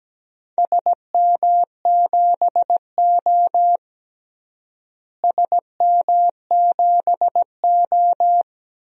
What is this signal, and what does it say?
Morse code, heard as a keyed tone, about 17 wpm, 700 Hz. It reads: SM7O SM7O
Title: DXCC-100 Call Signs - 4 Characters - Repeated 3x - 17wpm